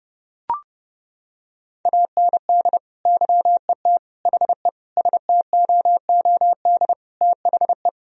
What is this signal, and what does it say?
Morse code, heard as a keyed tone, about 30 wpm, 700 Hz. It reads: AND YET HE STOOD THE
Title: Sentences from Top 1000 Words - Review Part 3 - 30wpm